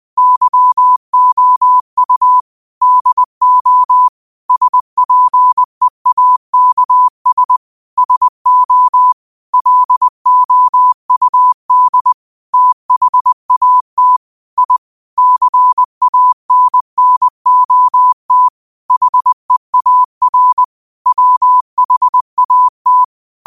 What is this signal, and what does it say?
Morse code, heard as a keyed tone, about 20 wpm, 1000 Hz. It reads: YOU DO SPEAKS SO LOUD THAT I CANNOT HEAR WHAT